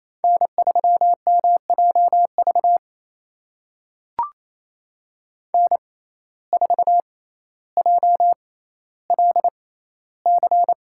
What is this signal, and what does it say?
Morse code, heard as a keyed tone, about 28 wpm, spaced as for 14 wpm, 700 Hz. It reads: N3MJV N4JLC
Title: USA Call Signs - 5 Characters - 6x Character Spacing - 28wpm